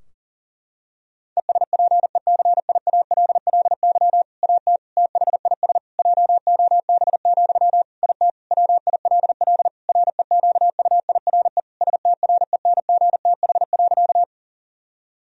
Morse code, heard as a keyed tone, about 40 wpm, 700 Hz. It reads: ESPECIALLY AT THIS JOB, IT WILL REQUIRE STRENGTH.
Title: Sentences from Top 1000 Words - Review Part 4 - 40wpm